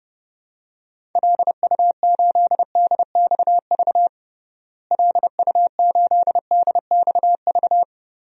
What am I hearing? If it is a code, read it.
Morse code, heard as a keyed tone, about 31 wpm, 700 Hz. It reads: LU8DXV LU8DXV
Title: DXCC-100 Call Signs - 6 Characters - Repeated 2x - 30wpm